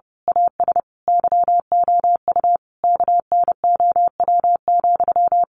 Morse code, heard as a keyed tone, about 30 wpm, 700 Hz. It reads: AS YOU KNOW,
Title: A Sound Of Thunder 30wpm